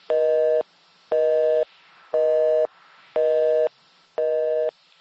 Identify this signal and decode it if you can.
Busy tone (North America), 480 + 620 Hz, with noise.
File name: phone numbers / calls / home 03